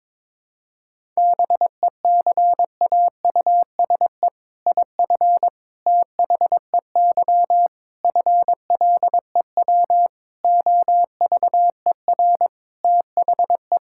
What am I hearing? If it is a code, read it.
Morse code, heard as a keyed tone, about 22 wpm, 700 Hz. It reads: BECAUSE IF THEY FLEW OVER THE
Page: morse-code-ninja-practice-22wpm